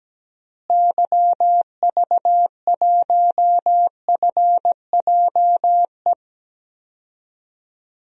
Morse code, heard as a keyed tone, about 17 wpm, 700 Hz. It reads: YV1FJE
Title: DXCC-100 Call Signs - 6 Characters - Repeated 3x - 17wpm